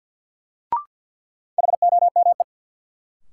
Morse code, heard as a keyed tone, about 53 wpm, 700 Hz. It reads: HOME
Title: Sentences from Top 200 Words - New Word Drill - 50wpm